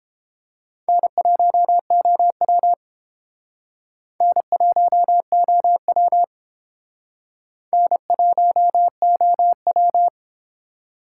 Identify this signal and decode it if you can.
Morse code, heard as a keyed tone, about 31 wpm, 700 Hz. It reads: N1OW N1OW N1OW